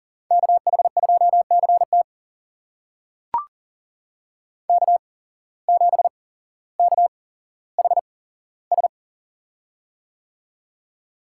Morse code, heard as a keyed tone, about 40 wpm, spaced as for 17 wpm, 700 Hz. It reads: KH2CT K7KHS
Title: USA Call Signs - 5 Characters - 8x Character Spacing - 40wpm